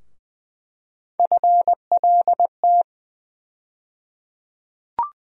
Morse code, heard as a keyed tone, about 20 wpm, 700 Hz. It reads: FLT
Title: USA Call Sign Postfixes - 3 Letters 20wpm